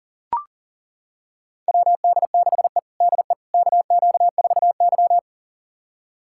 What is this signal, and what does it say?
Morse code, heard as a keyed tone, about 40 wpm, 700 Hz. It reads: WD6E DE KQ4Y
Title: USA Callsign DE Callsign - 4 Characters - 40wpm